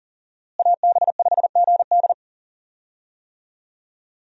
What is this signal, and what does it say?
Morse code, heard as a keyed tone, about 40 wpm, 700 Hz. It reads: AB5GD
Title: USA Call Signs - 5 Characters - Repeated 2x - 40wpm